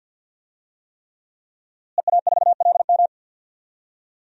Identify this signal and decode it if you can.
Morse code, heard as a keyed tone, about 52 wpm, 700 Hz. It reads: EA4LM
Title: DXCC-100 Call Signs - 5 Characters - Repeated 3x - 50wpm